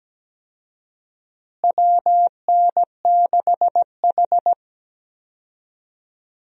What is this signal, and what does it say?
Morse code, heard as a keyed tone, about 17 wpm, 700 Hz. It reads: WN6H
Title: USA Call Signs - 4 Characters - Repeated 2x - 17wpm